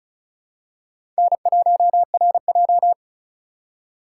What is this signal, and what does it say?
Morse code, heard as a keyed tone, about 36 wpm, 700 Hz. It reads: N1RJ
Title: USA Call Signs - 4 Characters - 6x Character Spacing - 35wpm